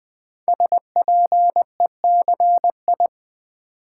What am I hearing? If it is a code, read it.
Morse code, heard as a keyed tone, about 20 wpm, 700 Hz. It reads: SPECI
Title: Build A Word - 5 Letters At A Time - 12-Letter Words - 20wpm